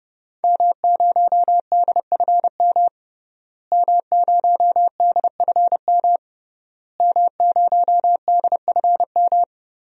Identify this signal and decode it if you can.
Morse code, heard as a keyed tone, about 30 wpm, 700 Hz. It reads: M0DFM M0DFM M0DFM
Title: DXCC-100 Call Signs - 5 Characters - Repeated 3x - 30wpm